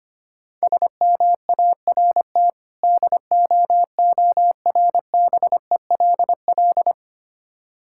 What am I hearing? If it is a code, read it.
Morse code, heard as a keyed tone, about 25 wpm, 690 Hz. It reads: SMART DOORBELL
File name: practice / 25wpm / Difficult Words with Hints - 25wpm